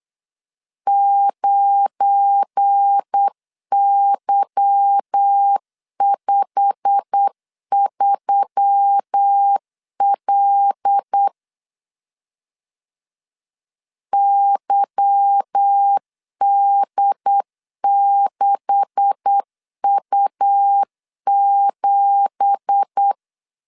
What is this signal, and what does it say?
Morse code, heard as a keyed tone, about 8 wpm, 800 Hz. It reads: 9Y53L YD6U7
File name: LEZIONI / 10 SIGNS / 10B INT 8wpm 3534